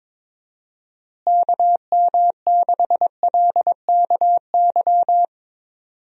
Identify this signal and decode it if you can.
Morse code, heard as a keyed tone, about 22 wpm, 700 Hz. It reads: KM6LKY
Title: USA Call Signs - 6 Characters - Repeated 2x - 22wpm